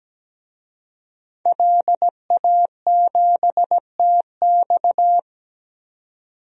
Morse code, heard as a keyed tone, about 17 wpm, 700 Hz. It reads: LA7TX
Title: DXCC-100 Call Signs - 5 Characters - 17wpm